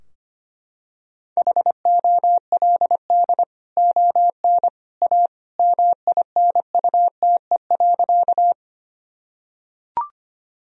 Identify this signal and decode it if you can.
Morse code, heard as a keyed tone, about 25 wpm, 700 Hz. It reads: HOLD ON A MINUTE.
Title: Sentences from Top 1000 Words - Review Part 2 - 25wpm